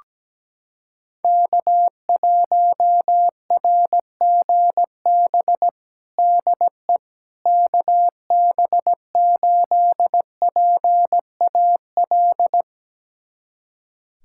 Morse code, heard as a keyed tone, about 17 wpm, 700 Hz. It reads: K1RGB DE KB8PAL
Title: USA Callsign DE Callsign 17wpm